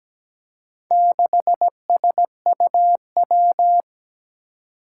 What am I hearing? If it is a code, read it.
Morse code, heard as a keyed tone, about 17 wpm, 700 Hz. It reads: 6SUW
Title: Four Challenging Letter-Number Combinations 17wpm